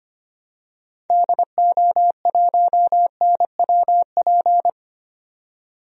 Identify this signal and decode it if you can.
Morse code, heard as a keyed tone, about 25 wpm, 700 Hz. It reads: DO1NWP
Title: DXCC-100 Call Signs - 6 Characters - 25wpm